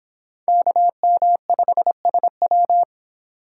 Morse code, heard as a keyed tone, about 26 wpm, 700 Hz. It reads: KM5SW